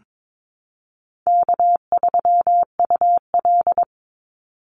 Morse code, heard as a keyed tone, about 22 wpm, 700 Hz. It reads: K3UL
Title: Easy Call Signs Repeated 3x - 22wpm